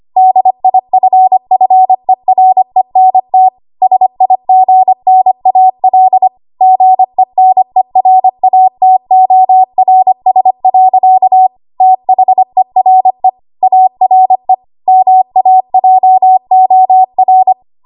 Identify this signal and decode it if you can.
Morse code, heard as a keyed tone, about 25 wpm, 750 Hz. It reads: DIFFERENT SIGNAL GENERATORS. THERE ARE MAJOR
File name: Archive / 25 WPM / 25WPM